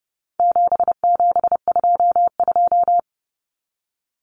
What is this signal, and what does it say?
Morse code, heard as a keyed tone, about 30 wpm, 700 Hz. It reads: 7722
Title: Numbers - 4-Digit 30wpm